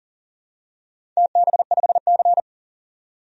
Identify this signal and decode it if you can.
Morse code, heard as a keyed tone, about 40 wpm, 700 Hz. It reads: TB5C
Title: DXCC-100 Call Signs - 4 Characters - 40wpm